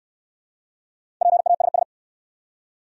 Morse code, heard as a keyed tone, about 68 wpm, 700 Hz. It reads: PASS